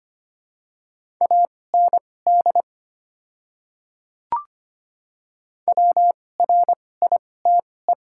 Morse code, heard as a keyed tone, about 25 wpm, spaced as for 18 wpm, 700 Hz. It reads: AND WRITE
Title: 2x Character Spacing - Top 100 Words - 25wpm